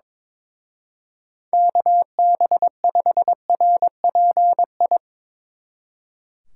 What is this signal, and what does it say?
Morse code, heard as a keyed tone, about 24 wpm, 700 Hz. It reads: KB5RPI
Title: Speed Racing - USA 2x3 Call Signs 22wpm